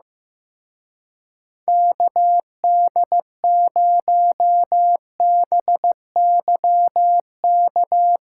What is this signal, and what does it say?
Morse code, heard as a keyed tone, about 16 wpm, 700 Hz. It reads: KD0BYK
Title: Speed Racing - USA 2x3 Call Signs 15wpm